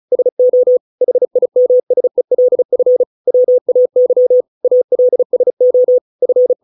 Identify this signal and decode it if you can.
Morse code, heard as a keyed tone, about 34 wpm, 500 Hz. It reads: SO HIMSELF WAY ALSO F